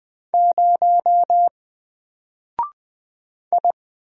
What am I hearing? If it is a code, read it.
Morse code, heard as a keyed tone, about 20 wpm, 700 Hz. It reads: 0 I
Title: Single Character (Letter:Number:Punctuation:Prosign) 20wpm